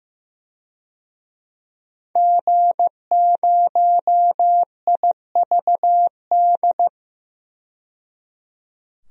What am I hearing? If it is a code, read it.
Morse code, heard as a keyed tone, about 15 wpm, 700 Hz. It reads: G0IVD
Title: DXCC-100 Call Signs - 5 Characters - 15wpm